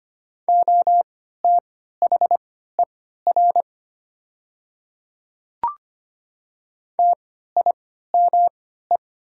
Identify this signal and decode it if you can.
Morse code, heard as a keyed tone, about 25 wpm, spaced as for 14 wpm, 700 Hz. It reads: OTHER TIME
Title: 3x Character Spacing - Top 200 Words - 25wpm